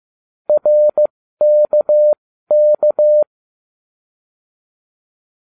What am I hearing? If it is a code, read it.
Morse code, heard as a keyed tone, about 15 wpm, spaced as for 13 wpm, 600 Hz. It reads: RKK